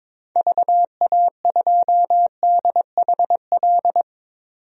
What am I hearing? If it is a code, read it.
Morse code, heard as a keyed tone, about 22 wpm, 700 Hz. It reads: VA2DHL
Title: DXCC-100 Call Signs - 6 Characters - 22wpm